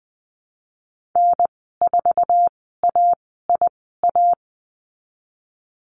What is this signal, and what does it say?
Morse code, heard as a keyed tone, about 20 wpm, spaced as for 15 wpm, 700 Hz. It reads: N4AIA
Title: 048 Lesson 9g - Number 4 - Call Signs 20wpm